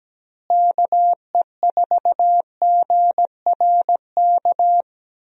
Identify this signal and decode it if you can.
Morse code, heard as a keyed tone, about 17 wpm, 700 Hz. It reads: KE4GRK
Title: USA Call Signs - 6 Characters - Repeated 2x - 17wpm